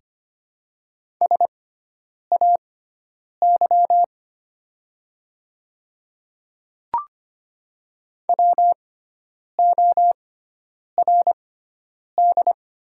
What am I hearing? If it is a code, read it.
Morse code, heard as a keyed tone, about 25 wpm, spaced as for 9 wpm, 690 Hz. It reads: SAY WORD
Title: 6x Character Spacing - Top 200 Words - 25wpm